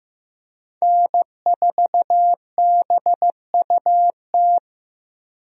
Morse code, heard as a keyed tone, about 15 wpm, 700 Hz. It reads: N4BUT
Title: USA Call Signs - 5 Characters - Repeated 3x - 15wpm